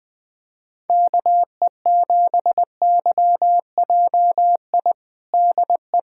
Morse code, heard as a keyed tone, about 20 wpm, 700 Hz. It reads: KE7YJI DE
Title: USA Callsign DE Callsign - 6 Characters - 20wpm